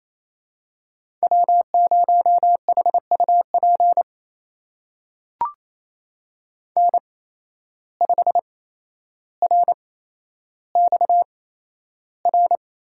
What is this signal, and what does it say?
Morse code, heard as a keyed tone, about 28 wpm, spaced as for 12 wpm, 700 Hz. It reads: W0HUP N5RXR
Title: USA Call Signs - 5 Characters - 8x Character Spacing - 28wpm